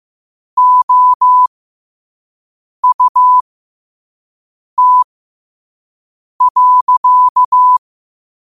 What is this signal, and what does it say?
Morse code, heard as a keyed tone, about 15 wpm, spaced as for 5 wpm, 1000 Hz. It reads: OUT.
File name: qotdfiles / 5wpm